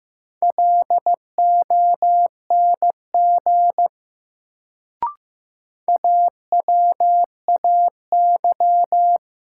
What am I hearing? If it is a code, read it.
Morse code, heard as a keyed tone, about 15 wpm, 700 Hz. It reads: LONG AWAY
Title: Sets of 2 Words - New Word Drill - 15wpm